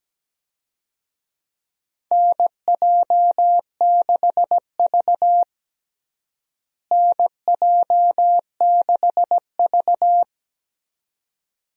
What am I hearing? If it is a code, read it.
Morse code, heard as a keyed tone, about 17 wpm, 700 Hz. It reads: NJ6V NJ6V
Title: USA Call Signs - 4 Characters - Repeated 2x - 17wpm